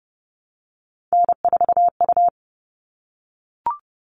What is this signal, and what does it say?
Morse code, heard as a keyed tone, about 30 wpm, 700 Hz. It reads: N4U
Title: All Cut Numbers - 3-Digit 30wpm